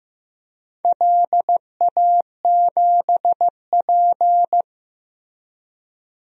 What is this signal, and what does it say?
Morse code, heard as a keyed tone, about 15 wpm, 700 Hz. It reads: LA7P
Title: DXCC-100 Call Signs - 4 Characters - Repeated 2x - 15wpm